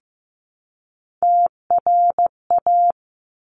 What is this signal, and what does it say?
Morse code, heard as a keyed tone, about 15 wpm, 700 Hz. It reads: TRA
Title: Top Three Letter Combinations in Words 15wpm